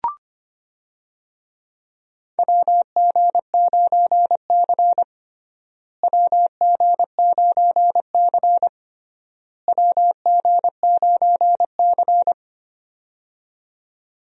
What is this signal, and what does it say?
Morse code, heard as a keyed tone, about 25 wpm, 700 Hz. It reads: WG9C WG9C WG9C
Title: USA Call Signs - 4 Characters - Repeated 3x - 25wpm